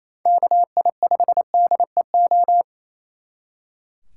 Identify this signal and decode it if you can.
Morse code, heard as a keyed tone, about 36 wpm, 700 Hz. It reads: KI5DEO